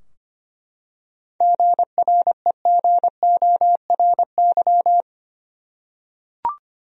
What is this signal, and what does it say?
Morse code, heard as a keyed tone, about 25 wpm, 700 Hz. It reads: GREGORY
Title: USA Common Names - 25wpm